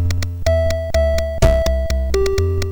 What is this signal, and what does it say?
There is no code.